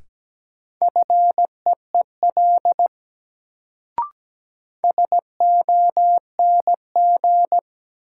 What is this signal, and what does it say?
Morse code, heard as a keyed tone, about 17 wpm, 700 Hz. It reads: FEEL SONG
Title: Sentences from Top 400 Words - New Word Drill - 17wpm